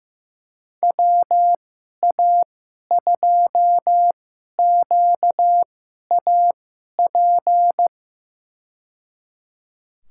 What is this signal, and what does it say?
Morse code, heard as a keyed tone, about 15 wpm, spaced as for 11 wpm, 700 Hz. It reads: WA2QAP
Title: USA Call Signs - 6 Characters - 2x Character Spacing - 15wpm